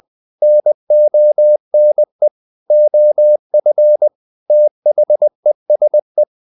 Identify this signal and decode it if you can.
Morse code, heard as a keyed tone, about 20 wpm, 600 Hz. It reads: NONE OF THESE
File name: the old man and the sea / the old man and the sea part 1 20wpm